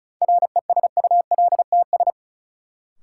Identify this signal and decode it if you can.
Morse code, heard as a keyed tone, about 36 wpm, 700 Hz. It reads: RESULTS
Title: Sets of 4 Words - New Word Drill - 35wpm